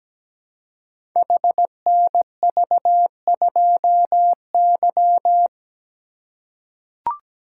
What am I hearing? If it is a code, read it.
Morse code, heard as a keyed tone, about 17 wpm, 700 Hz. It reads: HNV2Y
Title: Code Groups - 5 Letter:Numbers - 17wpm